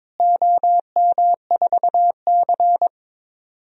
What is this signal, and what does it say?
Morse code, heard as a keyed tone, about 22 wpm, 700 Hz. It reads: OM4C
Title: DXCC-100 Call Signs - 4 Characters - Repeated 3x - 22wpm